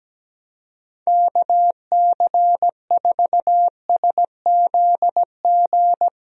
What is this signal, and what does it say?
Morse code, heard as a keyed tone, about 17 wpm, 700 Hz. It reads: KC4SZG
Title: USA Call Signs - 6 Characters - 17wpm